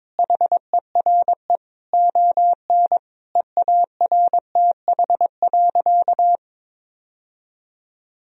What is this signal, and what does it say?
Morse code, heard as a keyed tone, about 22 wpm, 700 Hz. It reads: HERE ON EARTH.
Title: Sentences from Top 1000 Words - Review Part 2 - 22wpm